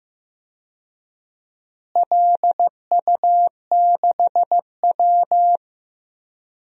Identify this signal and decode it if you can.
Morse code, heard as a keyed tone, about 15 wpm, 700 Hz. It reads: LU6W
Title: DXCC-100 Call Signs - 4 Characters - Repeated 2x - 15wpm